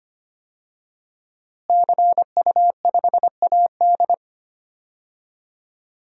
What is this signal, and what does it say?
Morse code, heard as a keyed tone, about 25 wpm, 700 Hz. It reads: CU5AD
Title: DXCC-100 Call Signs - 5 Characters - Repeated 2x - 25wpm